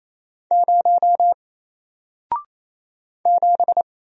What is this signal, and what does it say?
Morse code, heard as a keyed tone, about 29 wpm, 700 Hz. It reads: 0 7
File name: Single-Letter-Number-28wpm